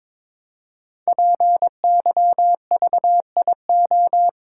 Morse code, heard as a keyed tone, about 22 wpm, 690 Hz. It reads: PYVIO
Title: Code Groups - 5 Letter:Numbers - 22wpm